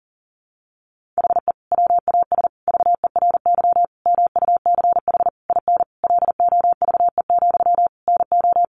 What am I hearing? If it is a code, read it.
Morse code, heard as a keyed tone, about 40 wpm, 700 Hz. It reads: HE WAS VERY MUCH IN LOVE, NO
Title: Sentences from Top 1500 Words 40wpm